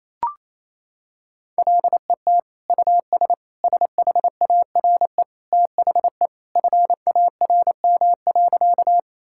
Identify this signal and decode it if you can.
Morse code, heard as a keyed tone, about 28 wpm, 700 Hz. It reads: LET US SHARE THE FARM.
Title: Sentences from Top 1000 Words - Review Part 6 - 28wpm